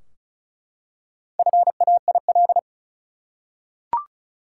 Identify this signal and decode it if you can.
Morse code, heard as a keyed tone, about 36 wpm, 700 Hz. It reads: FAIL
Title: Four Letter Words - 35wpm